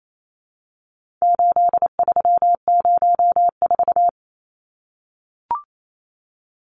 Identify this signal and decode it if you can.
Morse code, heard as a keyed tone, about 28 wpm, 700 Hz. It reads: 8304
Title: Numbers - 4-Digit 28wpm